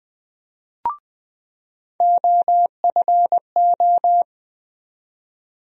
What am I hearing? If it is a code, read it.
Morse code, heard as a keyed tone, about 20 wpm, 700 Hz. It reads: OFO
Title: USA Call Sign Postfixes - 3 Letters 20wpm